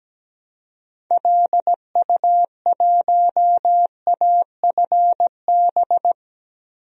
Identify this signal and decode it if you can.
Morse code, heard as a keyed tone, about 17 wpm, 700 Hz. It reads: LU1AFB
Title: DXCC-100 Call Signs - 6 Characters - Repeated 3x - 17wpm